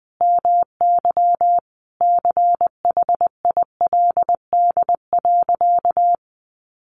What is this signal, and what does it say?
Morse code, heard as a keyed tone, about 20 wpm, 700 Hz. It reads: MY CHILD.
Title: Sentences from Top 1400 Words 20wpm